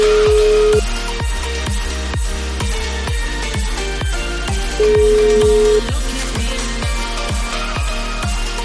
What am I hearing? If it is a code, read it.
Ringback tone (Europe, 425 Hz), with noise.